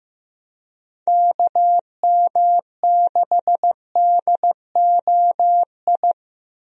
Morse code, heard as a keyed tone, about 15 wpm, 700 Hz. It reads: KM6DOI
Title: USA Call Signs - 6 Characters - Repeated 3x - 15wpm